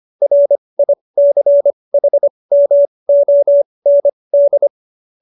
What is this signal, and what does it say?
Morse code, heard as a keyed tone, about 25 wpm, spaced as for 20 wpm, 570 Hz. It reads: RICHMOND